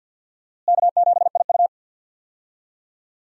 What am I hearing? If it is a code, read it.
Morse code, heard as a keyed tone, about 52 wpm, 700 Hz. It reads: K7IU
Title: USA Call Signs - 4 Characters - Repeated 2x - 50wpm